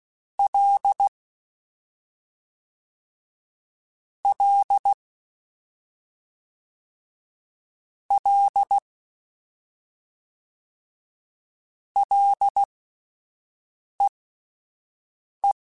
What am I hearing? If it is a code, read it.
Morse code, heard as a keyed tone, about 16 wpm, spaced as for 5 wpm, 800 Hz. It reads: L L L LEE
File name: Ls / Single L m3s / Single L Prefixe 3x2 Twice @5